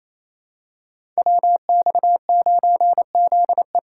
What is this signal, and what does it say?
Morse code, heard as a keyed tone, about 27 wpm, 700 Hz. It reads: WX9ZE
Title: Code Groups - 5 Letter:Numbers - 28wpm